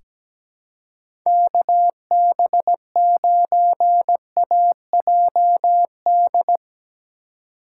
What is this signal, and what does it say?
Morse code, heard as a keyed tone, about 17 wpm, 700 Hz. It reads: KB9AJD
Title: USA Call Signs - 2x Character Spacing - 17wpm